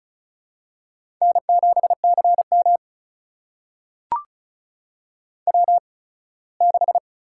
Morse code, heard as a keyed tone, about 36 wpm, spaced as for 21 wpm, 700 Hz. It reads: N7CM W6
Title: USA Call Signs - 8x Character Spacing - 35wpm